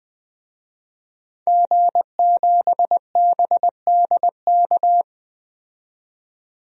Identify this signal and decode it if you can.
Morse code, heard as a keyed tone, about 20 wpm, 700 Hz. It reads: G7BDK
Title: DXCC-100 Call Signs - 5 Characters - 20wpm